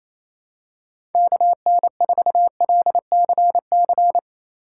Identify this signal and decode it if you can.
Morse code, heard as a keyed tone, about 27 wpm, 700 Hz. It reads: KN4LCC
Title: USA Call Signs - 6 Characters - 8x Character Spacing - 28wpm